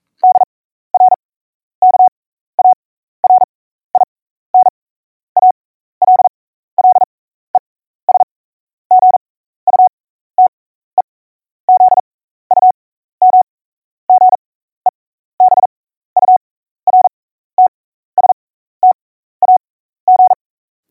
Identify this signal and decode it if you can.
Morse code, heard as a keyed tone, about 42 wpm, spaced as for 15 wpm, 740 Hz. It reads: DR KARIN ALLES GUTE ZUM GEBURTSTAG
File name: oe1-bulletin / OE1-Bulletin-49